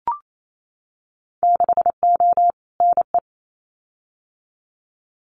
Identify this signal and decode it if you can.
Morse code, heard as a keyed tone, about 28 wpm, 700 Hz. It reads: 6O NE
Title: Winter Field Day Exchanges Encore 28wpm